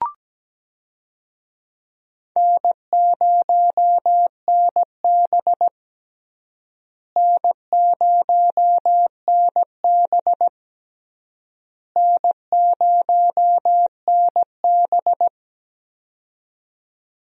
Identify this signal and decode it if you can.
Morse code, heard as a keyed tone, about 17 wpm, 700 Hz. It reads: N0NB N0NB N0NB
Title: USA Call Signs - 4 Characters - Repeated 3x - 17wpm